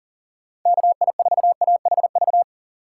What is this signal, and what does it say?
Morse code, heard as a keyed tone, about 40 wpm, 700 Hz. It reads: KI4AHV
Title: USA Call Signs - 6 Characters - 4x Character Spacing - 40wpm